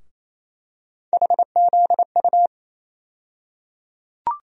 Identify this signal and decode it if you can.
Morse code, heard as a keyed tone, about 28 wpm, 700 Hz. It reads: HZU
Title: USA Call Sign Postfixes - 3 Letters 28wpm